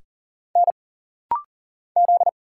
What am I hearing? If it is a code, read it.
Morse code, heard as a keyed tone, about 40 wpm, 700 Hz. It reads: N Z
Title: Single Letters - Rapid-Fire 40wpm